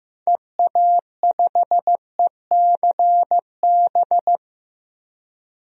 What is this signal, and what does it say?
Morse code, heard as a keyed tone, about 15 wpm, 700 Hz. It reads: EA5ECB
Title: DXCC-100 Call Signs - 6 Characters - 15wpm